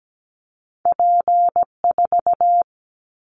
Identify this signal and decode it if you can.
Morse code, heard as a keyed tone, about 17 wpm, 700 Hz. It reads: P4
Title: Callsign Prefixes for Top 100 DXCC - 17wpm